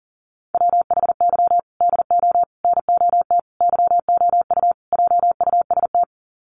Morse code, heard as a keyed tone, about 40 wpm, 700 Hz. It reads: WHY DO NOT YOU JUST
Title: Sets of 5 Words 40wpm